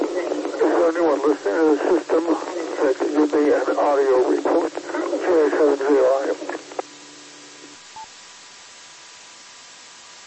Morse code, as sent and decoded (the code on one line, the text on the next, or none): none